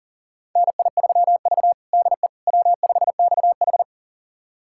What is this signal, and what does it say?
Morse code, heard as a keyed tone, about 40 wpm, 700 Hz. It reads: NI3V DE W5XH
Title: USA Callsign DE Callsign - 4 Characters - 40wpm